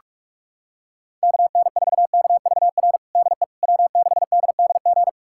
Morse code, heard as a keyed tone, about 42 wpm, 700 Hz. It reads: KN4KVR DE W6DDG
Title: USA Callsign DE Callsign 45wpm